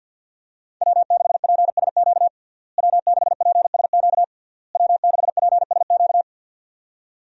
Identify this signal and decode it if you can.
Morse code, heard as a keyed tone, about 52 wpm, 700 Hz. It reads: W6PSQ W6PSQ W6PSQ
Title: USA Call Signs - 5 Characters - Repeated 3x - 50wpm